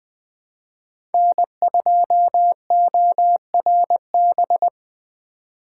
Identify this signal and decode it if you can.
Morse code, heard as a keyed tone, about 21 wpm, 700 Hz. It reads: N2ORB
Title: Speed Racing - USA 1x3 Call Signs 17wpm